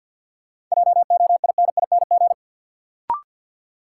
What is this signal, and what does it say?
Morse code, heard as a keyed tone, about 53 wpm, 700 Hz. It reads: JOINING
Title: Sets of 5 Words - New Word Drill - 50wpm